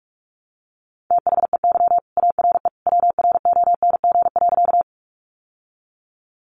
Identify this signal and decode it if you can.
Morse code, heard as a keyed tone, about 44 wpm, 700 Hz. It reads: THEY ARE WRONG.
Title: Sentences from Top 900 Words 45wpm